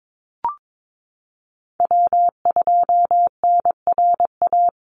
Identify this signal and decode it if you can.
Morse code, heard as a keyed tone, about 22 wpm, 700 Hz. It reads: W2NRA
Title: Call Signs Repeated 3x - 22wpm